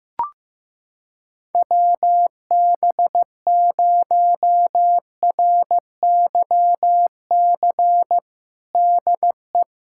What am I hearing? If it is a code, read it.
Morse code, heard as a keyed tone, about 15 wpm, 700 Hz. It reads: WB0RYC DE
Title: USA Callsign DE Callsign 15wpm